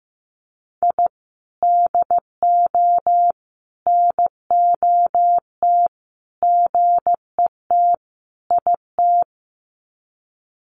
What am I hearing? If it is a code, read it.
Morse code, heard as a keyed tone, about 15 wpm, 700 Hz. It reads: I DO NOT GET IT
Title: Sets of 5 Words Encore 15wpm